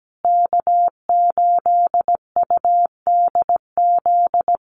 Morse code, heard as a keyed tone, about 17 wpm, 700 Hz. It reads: K8UDZ
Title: USA Call Signs Repeated 3x 17wpm